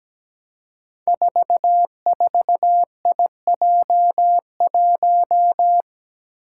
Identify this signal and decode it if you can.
Morse code, heard as a keyed tone, about 17 wpm, 700 Hz. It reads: 44IJ1
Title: Five Challenging Letter-Number Combinations 17wpm